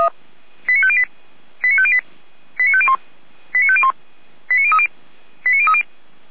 Five-tone selective call (ZVEI), heard as quick runs of five tones, tones of about 70 ms. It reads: 89508 89508 89501 89501 89030 89030